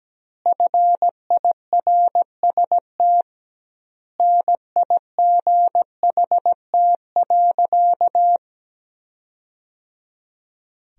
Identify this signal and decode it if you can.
Morse code, heard as a keyed tone, about 17 wpm, 700 Hz. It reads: FIRST NIGHT.
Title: 2x Word-Spacing - Sentences from Top 300 Words - 17wpm